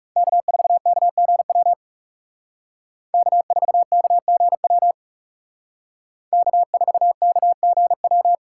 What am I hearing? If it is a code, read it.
Morse code, heard as a keyed tone, about 36 wpm, 700 Hz. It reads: K4KGW K4KGW K4KGW
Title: Speed Racing - USA 1x3 Call Signs 30wpm